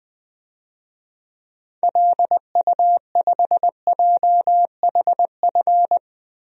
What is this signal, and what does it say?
Morse code, heard as a keyed tone, about 20 wpm, 700 Hz. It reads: LU5JHF
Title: DXCC-100 Call Signs - 6 Characters - 20wpm